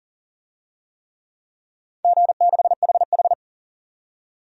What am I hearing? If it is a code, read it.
Morse code, heard as a keyed tone, about 40 wpm, 700 Hz. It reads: G6HH
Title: DXCC-100 Call Signs - 4 Characters - 40wpm